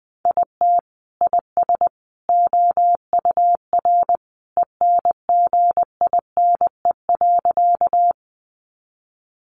Morse code, heard as a keyed tone, about 20 wpm, 700 Hz. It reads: IT IS OUR ENGINE.
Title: Sentences from Top 1500 Words 20wpm